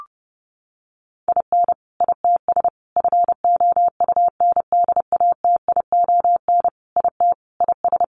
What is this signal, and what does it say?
Morse code, heard as a keyed tone, about 30 wpm, 700 Hz. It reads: IN ITS FOUNDATION IT IS
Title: Sentences from Top 1200 Words 30wpm